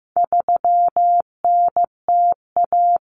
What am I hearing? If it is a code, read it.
Morse code, heard as a keyed tone, about 15 wpm, 700 Hz. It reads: 3NTA
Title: Common Cut Numbers - 4-Digit 15wpm